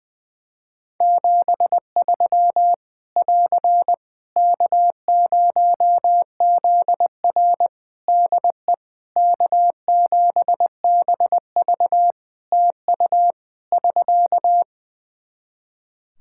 Morse code, heard as a keyed tone, about 20 wpm, 700 Hz. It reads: 73 <AR> K0ZR DE K7BV TU <SK>
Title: K1USN SST QSO Practice 20wpm